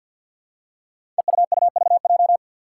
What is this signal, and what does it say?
Morse code, heard as a keyed tone, about 52 wpm, 700 Hz. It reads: EUUVJ
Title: Five Challenging Letter-Number Combinations 50wpm